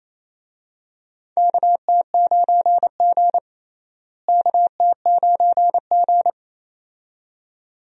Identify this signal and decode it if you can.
Morse code, heard as a keyed tone, about 27 wpm, 700 Hz. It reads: KT9G KT9G
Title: USA Call Signs - 4 Characters - Repeated 2x - 28wpm